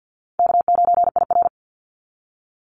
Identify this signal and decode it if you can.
Morse code, heard as a keyed tone, about 48 wpm, 700 Hz. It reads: K9IR
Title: NA SOTA Activator Callsigns 50wpm